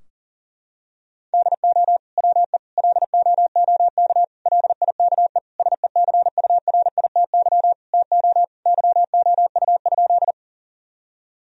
Morse code, heard as a keyed tone, about 40 wpm, 700 Hz. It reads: DO WE LOOK LIKE SECURITY TO YOU?
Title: Sentences from Top 1000 Words - Review Part 4 - 40wpm